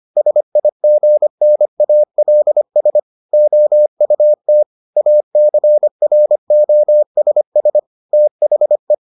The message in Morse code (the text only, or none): SIGNALS OUT ACROSS THE